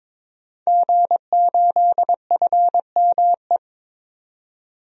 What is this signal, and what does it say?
Morse code, heard as a keyed tone, about 21 wpm, 700 Hz. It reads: G8FME
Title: DXCC-100 Call Signs - 5 Characters - 22wpm